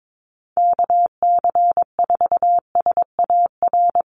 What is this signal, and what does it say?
Morse code, heard as a keyed tone, about 22 wpm, 700 Hz. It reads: KC4SAR
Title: USA Call Signs 22wpm